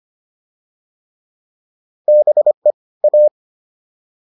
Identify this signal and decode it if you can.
Morse code, heard as a keyed tone, about 25 wpm, 600 Hz. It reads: BE A